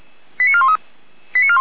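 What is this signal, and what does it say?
Five-tone selective call (ZVEI), heard as quick runs of five tones, tones of about 70 ms. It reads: 89413
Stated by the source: ZVEI five-tone selective call received by SDR.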